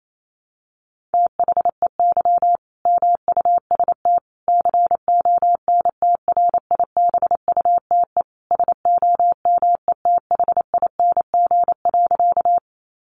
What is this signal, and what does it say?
Morse code, heard as a keyed tone, about 28 wpm, 700 Hz. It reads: THEY MUST CONTRIBUTE SOMETHING.
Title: Sentences from Top 1400 Words 28wpm